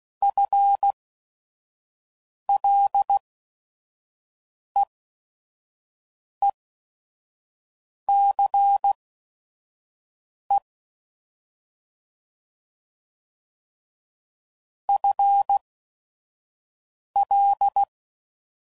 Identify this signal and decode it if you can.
Morse code, heard as a keyed tone, about 16 wpm, spaced as for 5 wpm, 800 Hz. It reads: FLEECE FL
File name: Prefixes / FLs & THs / FL Prefixes 1